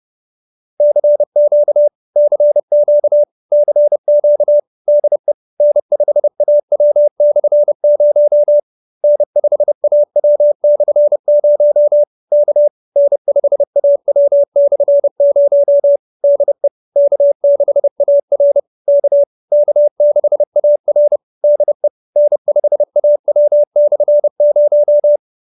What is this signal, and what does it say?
Morse code, heard as a keyed tone, about 30 wpm, 590 Hz. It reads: CQ CQ CQ DE N5AW/0 N5AW/0 K N5AW/0 DE K6AR K K6AR DE N5AW/0